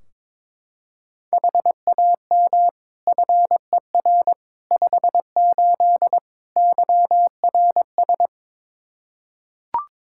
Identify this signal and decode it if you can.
Morse code, heard as a keyed tone, about 23 wpm, 700 Hz. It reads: HAM FER 58 YRS
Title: Practice QSO - Exchange 3 - More Info - 22wpm